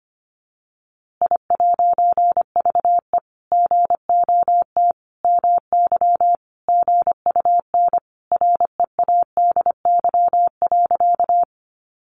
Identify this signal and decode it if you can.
Morse code, heard as a keyed tone, about 25 wpm, 700 Hz. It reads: I'VE GOT MY GUN READY.
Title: A Sound Of Thunder 25wpm